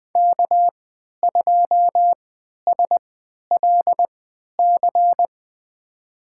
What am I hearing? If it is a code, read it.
Morse code, heard as a keyed tone, about 20 wpm, spaced as for 11 wpm, 700 Hz. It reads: K2SLC
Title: USA Call Signs - 5 Characters - 3x Character Spacing - 20wpm